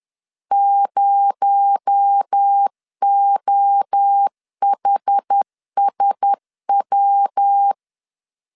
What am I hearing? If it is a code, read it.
Morse code, heard as a keyed tone, about 10 wpm, 800 Hz. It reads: 0OHSW
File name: CW / DET / 1 6 1500C S B 10wpm 4137